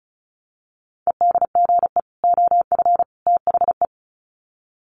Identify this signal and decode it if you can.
Morse code, heard as a keyed tone, about 34 wpm, 700 Hz. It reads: EDGE OF THE
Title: Sets of 3 Words 35wpm